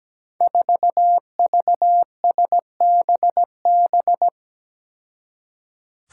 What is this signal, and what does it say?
Morse code, heard as a keyed tone, about 17 wpm, 700 Hz. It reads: 4VSBB
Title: Five Challenging Letter-Number Combinations 17wpm